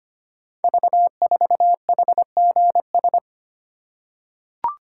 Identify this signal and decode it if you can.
Morse code, heard as a keyed tone, about 25 wpm, 700 Hz. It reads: V4HGS
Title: Code Groups - 5 Letter:Numbers - 25wpm